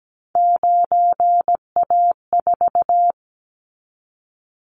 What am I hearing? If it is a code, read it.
Morse code, heard as a keyed tone, about 17 wpm, 700 Hz. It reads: 9A4
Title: Callsign Prefixes for Top 100 DXCC - 17wpm